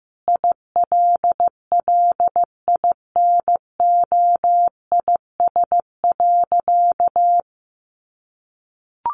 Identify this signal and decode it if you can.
Morse code, heard as a keyed tone, about 15 wpm, 700 Hz. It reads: ILLINOIS.
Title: A Sound Of Thunder 15wpm